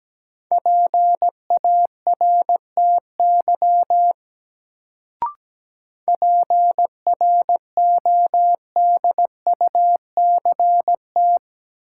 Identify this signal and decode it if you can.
Morse code, heard as a keyed tone, about 17 wpm, 700 Hz. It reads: PARTY PRODUCT
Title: Sentences from Top 1400 Words - New Word Drill Alt - 17wpm